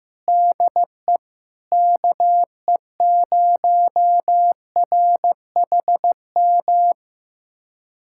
Morse code, heard as a keyed tone, about 15 wpm, 700 Hz. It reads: DE KE0RHM
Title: USA Callsign DE Callsign - 6 Characters - 15wpm